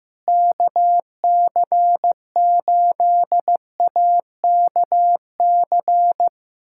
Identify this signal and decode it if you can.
Morse code, heard as a keyed tone, about 16 wpm, 700 Hz. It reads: KC8AKC
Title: USA Callsign DE Callsign - 6 Characters - 15wpm